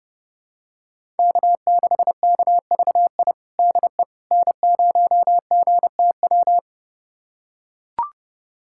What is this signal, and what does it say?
Morse code, heard as a keyed tone, about 30 wpm, 700 Hz. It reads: K6KVI DE N0GTW
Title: USA Callsign DE Callsign - 5 Characters - 30wpm